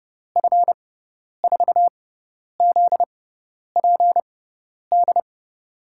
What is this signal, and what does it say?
Morse code, heard as a keyed tone, about 30 wpm, spaced as for 10 wpm, 700 Hz. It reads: F4ZPD
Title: USA Call Signs - 6x Character Spacing - 30wpm